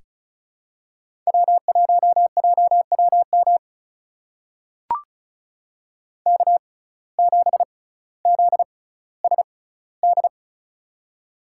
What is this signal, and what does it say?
Morse code, heard as a keyed tone, about 36 wpm, spaced as for 18 wpm, 700 Hz. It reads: W1JWM K7ZSD
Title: USA Call Signs - 6x Character Spacing - 35wpm